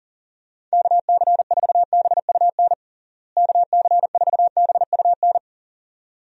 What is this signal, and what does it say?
Morse code, heard as a keyed tone, about 40 wpm, 700 Hz. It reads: KC4BUN KC4BUN
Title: USA Call Signs - 6 Characters - Repeated 2x - 40wpm